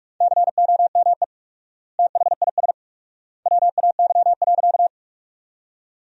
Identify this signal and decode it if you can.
Morse code, heard as a keyed tone, about 41 wpm, 700 Hz. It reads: COME THIS WAY.
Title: 4x Word-Spacing - Sentences from Top 100 Words - 45wpm